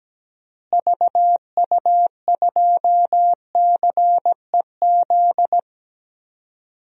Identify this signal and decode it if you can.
Morse code, heard as a keyed tone, about 17 wpm, 700 Hz. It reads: VU2CEZ
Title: DXCC-100 Call Signs - 6 Characters - Repeated 2x - 17wpm